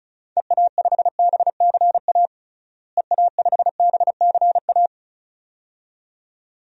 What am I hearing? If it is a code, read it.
Morse code, heard as a keyed tone, about 36 wpm, 700 Hz. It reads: EA5BCA EA5BCA
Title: DXCC-100 Call Signs - 6 Characters - Repeated 2x - 35wpm